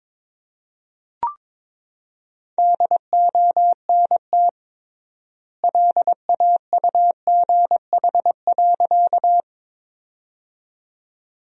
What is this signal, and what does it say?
Morse code, heard as a keyed tone, about 22 wpm, 700 Hz. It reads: DONT LAUGH.
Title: 3x Word-Spacing - Sentences from Top 500 Words - 22wpm